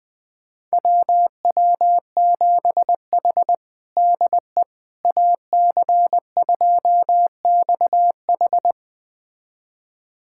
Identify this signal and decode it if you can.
Morse code, heard as a keyed tone, about 20 wpm, 700 Hz. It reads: WW7H DE AC2XH
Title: USA Callsign DE Callsign 20wpm